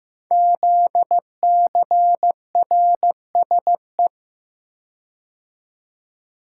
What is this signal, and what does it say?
Morse code, heard as a keyed tone, about 15 wpm, 700 Hz. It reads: ZCRSE
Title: Code Groups - 5 Letters - 15wpm